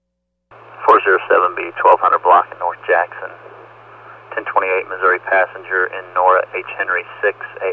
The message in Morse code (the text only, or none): none